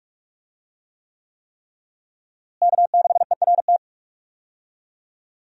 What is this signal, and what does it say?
Morse code, heard as a keyed tone, about 43 wpm, 700 Hz. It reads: K6ERT
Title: USA Call Signs - 5 Characters - Repeated 3x - 45wpm